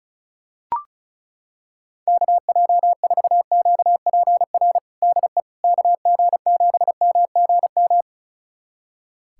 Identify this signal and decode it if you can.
Morse code, heard as a keyed tone, about 36 wpm, 700 Hz. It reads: KJ4QPR DE KG7MGM
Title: USA Callsign DE Callsign 35wpm